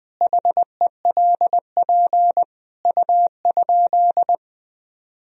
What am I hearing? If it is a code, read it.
Morse code, heard as a keyed tone, about 21 wpm, 700 Hz. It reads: HELP U?
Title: Sentences from Top 200 Words and Abbreviations 20wpm